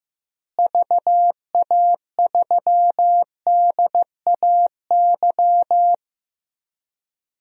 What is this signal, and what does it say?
Morse code, heard as a keyed tone, about 15 wpm, 700 Hz. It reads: VA3DAY
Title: DXCC-100 Call Signs - 6 Characters - Repeated 3x - 15wpm